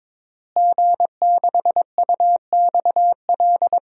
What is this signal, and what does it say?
Morse code, heard as a keyed tone, about 22 wpm, 700 Hz. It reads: G6UXL
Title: DXCC-100 Call Signs - 5 Characters - Repeated 3x - 22wpm